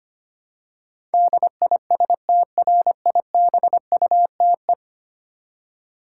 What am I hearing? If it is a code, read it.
Morse code, heard as a keyed tone, about 25 wpm, 700 Hz. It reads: DISTRIBUTE
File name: Ten Letter Words - 25wpm